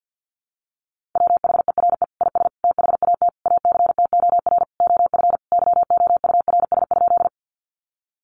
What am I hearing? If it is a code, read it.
Morse code, heard as a keyed tone, about 48 wpm, 700 Hz. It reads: WHERE IS THAT ACTOR OF YOURS?
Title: Sentences from Top 1500 Words 50wpm